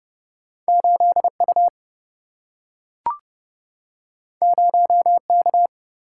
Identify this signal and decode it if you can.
Morse code, heard as a keyed tone, about 30 wpm, 700 Hz. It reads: 8U 0K
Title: Single-Letter Single-Number 30wpm